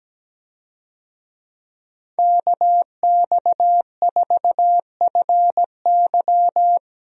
Morse code, heard as a keyed tone, about 17 wpm, 700 Hz. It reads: KX4FY
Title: USA Call Signs - 5 Characters - 17wpm